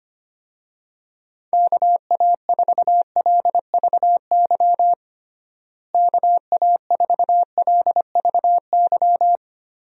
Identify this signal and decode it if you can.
Morse code, heard as a keyed tone, about 25 wpm, 700 Hz. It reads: KA4LVY KA4LVY
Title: USA Call Signs - 6 Characters - Repeated 2x - 25wpm